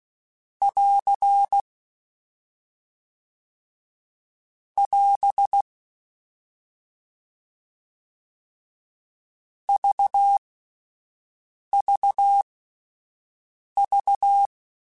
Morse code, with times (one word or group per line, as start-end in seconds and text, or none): 0.6-1.6 <AR>
4.8-5.6 <AS>
9.7-14.5 VVV